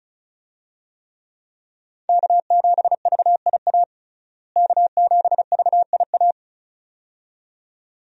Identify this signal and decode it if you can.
Morse code, heard as a keyed tone, about 36 wpm, 700 Hz. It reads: K7VIA K7VIA
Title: USA Call Signs - 5 Characters - Repeated 2x - 35wpm